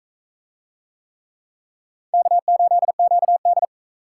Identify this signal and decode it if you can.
Morse code, heard as a keyed tone, about 44 wpm, 700 Hz. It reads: K8QD